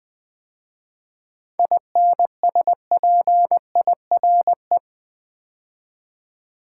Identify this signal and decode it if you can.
Morse code, heard as a keyed tone, about 20 wpm, 700 Hz. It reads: INSPIRE
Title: 7 Letter Words - First 3 Letters Given - 20wpm